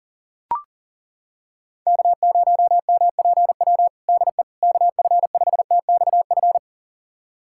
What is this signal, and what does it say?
Morse code, heard as a keyed tone, about 37 wpm, 700 Hz. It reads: K0MPW DE KF5TXF
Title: USA Callsign DE Callsign 40wpm